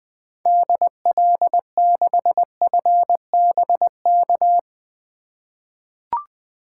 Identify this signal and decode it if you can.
Morse code, heard as a keyed tone, about 21 wpm, 700 Hz. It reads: DL6FBK
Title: 2023 SOTA Activator Call Signs - 20wpm